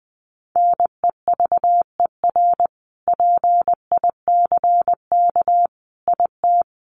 Morse code, heard as a keyed tone, about 20 wpm, 700 Hz. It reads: NEVER PICK IT
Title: Sentences from Top 1000 Words 20wpm